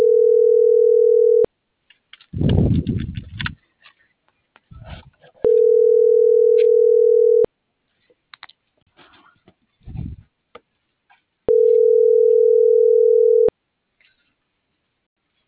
Ringback tone (North America), 440 + 480 Hz.